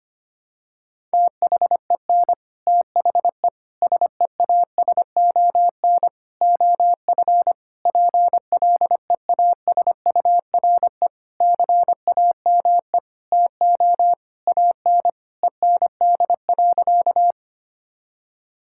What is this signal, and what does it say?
Morse code, heard as a keyed tone, about 25 wpm, 700 Hz. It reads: THEN THE SEASON OF PLEASURE CAME TO AN END.
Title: Sentences from Top 1600 Words 25wpm